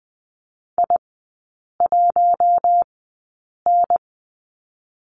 Morse code, heard as a keyed tone, about 20 wpm, 700 Hz. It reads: I 1 N
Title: 127 Lesson 21e - Number 6 - Three Characters 20wpm